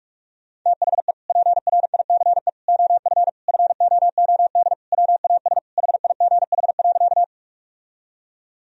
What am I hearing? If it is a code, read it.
Morse code, heard as a keyed tone, about 46 wpm, 700 Hz. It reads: THE PRICE OF FOOD WAS HIGH.
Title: Sentences from Top 1000 Words - Review Part 4 - 45wpm